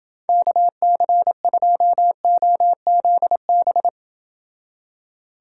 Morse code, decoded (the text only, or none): KC2OZB